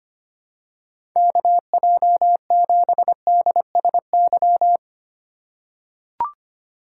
Morse code, heard as a keyed tone, about 25 wpm, 700 Hz. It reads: KJ7DSY
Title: USA Call Signs - 2x Character Spacing - 25wpm